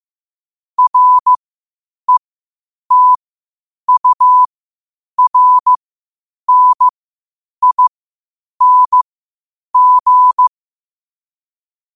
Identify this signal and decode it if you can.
Morse code, heard as a keyed tone, about 15 wpm, spaced as for 8 wpm, 1000 Hz. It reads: RETURNING
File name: qotdfiles / 8wpm